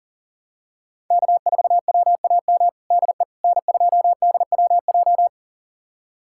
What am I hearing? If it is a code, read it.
Morse code, heard as a keyed tone, about 40 wpm, 700 Hz. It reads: K4WAM DE N2DWJ
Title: USA Callsign DE Callsign - 5 Characters - 40wpm